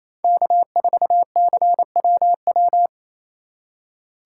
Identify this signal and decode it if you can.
Morse code, heard as a keyed tone, about 28 wpm, 700 Hz. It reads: K4CWW
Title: USA Call Signs - 5 Characters - 4x Character Spacing - 28wpm